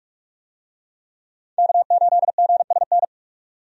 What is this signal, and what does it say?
Morse code, heard as a keyed tone, about 42 wpm, 700 Hz. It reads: K8GSN
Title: USA Call Signs - 5 Characters - 45wpm